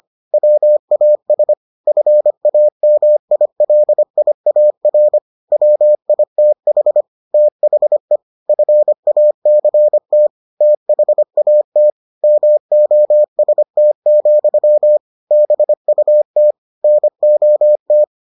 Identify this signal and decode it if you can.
Morse code, heard as a keyed tone, about 25 wpm, 600 Hz. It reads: WAS FAMILIAR WITH THE FACT THAT MOST, BUT NOT